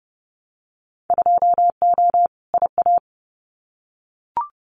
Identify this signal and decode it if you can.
Morse code, heard as a keyed tone, about 30 wpm, 700 Hz. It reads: 2O IA
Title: Winter Field Day Exchanges 30wpm